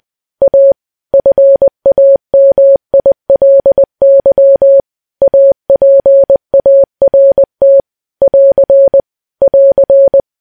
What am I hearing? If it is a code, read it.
Morse code, heard as a keyed tone, about 20 wpm, 550 Hz. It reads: A FAMILY APART <AR> <AR>
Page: news-headlines-in-morse-code-at-20-wpm